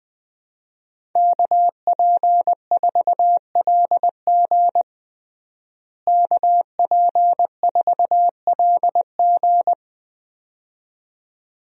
Morse code, heard as a keyed tone, about 20 wpm, 700 Hz. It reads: KP4LG KP4LG
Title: DXCC-100 Call Signs - 5 Characters - Repeated 2x - 20wpm